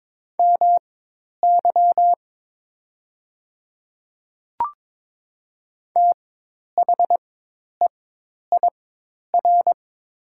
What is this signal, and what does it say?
Morse code, heard as a keyed tone, about 22 wpm, spaced as for 10 wpm, 700 Hz. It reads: MY THEIR
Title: 4x Character Spacing - Top 100 Words - 22wpm